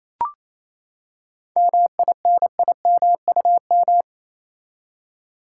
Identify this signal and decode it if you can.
Morse code, heard as a keyed tone, about 28 wpm, 700 Hz. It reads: MINIMUM
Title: Seven Letter Words - 28wpm